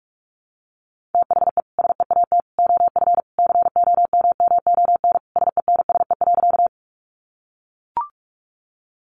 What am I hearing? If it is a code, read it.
Morse code, heard as a keyed tone, about 44 wpm, 700 Hz. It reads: THE SEAT OF COMMON SENSE.
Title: Sentences from Top 1500 Words 45wpm